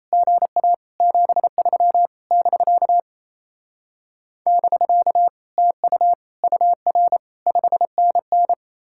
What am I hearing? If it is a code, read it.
Morse code, heard as a keyed tone, about 36 wpm, 700 Hz. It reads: GA 73 <BK> <BK> TU UR 5NN